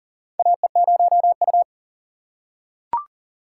Morse code, heard as a keyed tone, about 38 wpm, 700 Hz. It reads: AE0U
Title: USA Call Signs - 4 Characters - 3x Character Spacing - 40wpm